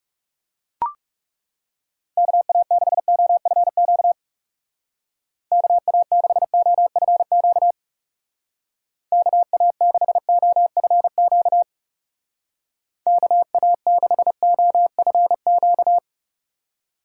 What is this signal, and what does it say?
Morse code, heard as a keyed tone, about 36 wpm, 700 Hz. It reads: KA6OFQ KA6OFQ KA6OFQ KA6OFQ
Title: Speed Racing - USA 2x3 Call Signs 30wpm